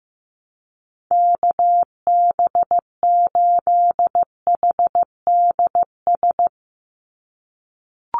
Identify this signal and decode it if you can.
Morse code, heard as a keyed tone, about 15 wpm, 700 Hz. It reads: KB8HDS
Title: USA Call Signs 15wpm